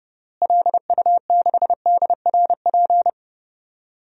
Morse code, heard as a keyed tone, about 31 wpm, 700 Hz. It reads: LU6DRP
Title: DXCC-100 Call Signs - 6 Characters - Repeated 3x - 30wpm